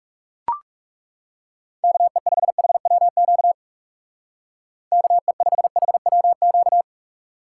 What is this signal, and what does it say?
Morse code, heard as a keyed tone, about 36 wpm, 700 Hz. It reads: KE5HWQ KE5HWQ
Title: Speed Racing - USA 2x3 Call Signs 30wpm